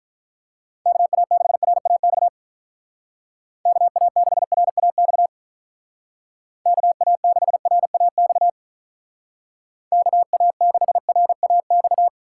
Morse code, heard as a keyed tone, about 44 wpm, 700 Hz. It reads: KA6RAX KA6RAX KA6RAX KA6RAX